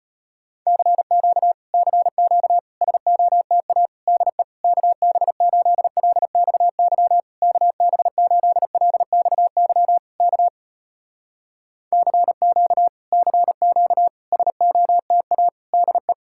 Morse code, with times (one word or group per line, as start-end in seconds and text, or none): 0.7-1.5 CQ
1.7-2.6 CQ
2.8-3.9 SOTA
4.1-4.4 DE
4.6-7.2 KB8LXY
7.4-10.0 KB8LXY
10.2-10.5 K
11.9-12.9 CQ
13.1-14.1 CQ
14.3-15.5 SOTA
15.7-16.1 DE